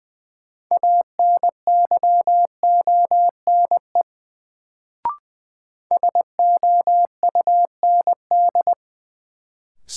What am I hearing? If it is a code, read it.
Morse code, heard as a keyed tone, about 20 wpm, 700 Hz. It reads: ANYONE SOUND
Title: Sentences from Top 900 Words - New Word Drill - 20wpm